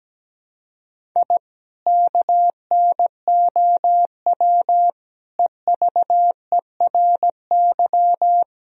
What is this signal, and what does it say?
Morse code, heard as a keyed tone, about 17 wpm, 700 Hz. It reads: I KNOW EVERY
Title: Sentences from Top 1000 Words - Review Part 3 - 17wpm